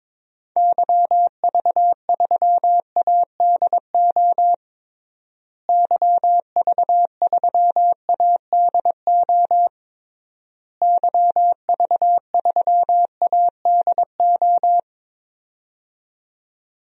Morse code, heard as a keyed tone, about 22 wpm, 700 Hz. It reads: YV3ADO YV3ADO YV3ADO
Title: DXCC-100 Call Signs - 6 Characters - Repeated 3x - 22wpm